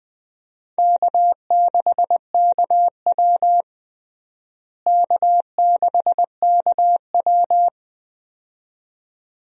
Morse code, heard as a keyed tone, about 21 wpm, 700 Hz. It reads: K6KW K6KW
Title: USA Call Signs - 4 Characters - Repeated 2x - 20wpm